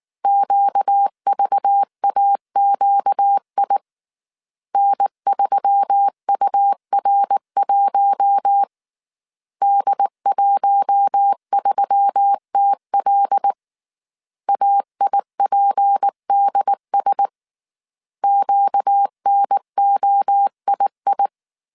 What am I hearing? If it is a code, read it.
Morse code, heard as a keyed tone, about 19 wpm, 800 Hz. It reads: QVAQI N3UR1 D13TL AIPDS QNOII